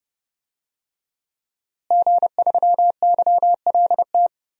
Morse code, heard as a keyed tone, about 29 wpm, 700 Hz. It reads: G3YLT
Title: DXCC-100 Call Signs - 5 Characters - Repeated 2x - 30wpm